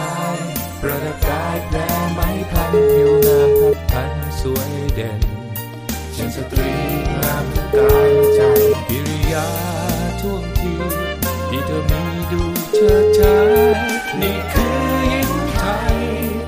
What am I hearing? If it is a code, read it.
Ringback tone (Europe, 425 Hz), with noise.